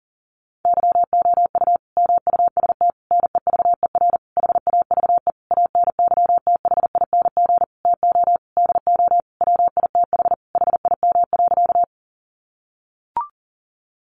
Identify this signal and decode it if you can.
Morse code, heard as a keyed tone, about 40 wpm, 700 Hz. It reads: YOU MUST NEVER HAVE ANYTHING TO DO WITH HIM.
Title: Sentences from Top 1000 Words 40wpm